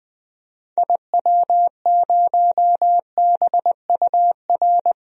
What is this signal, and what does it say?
Morse code, heard as a keyed tone, about 20 wpm, 700 Hz. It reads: IW0BUR
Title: DXCC-100 Call Signs - 6 Characters - Repeated 3x - 20wpm